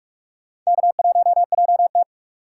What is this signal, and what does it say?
Morse code, heard as a keyed tone, about 44 wpm, 690 Hz. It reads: K1JT
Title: USA Call Signs - 4 Characters - 2x Character Spacing - 45wpm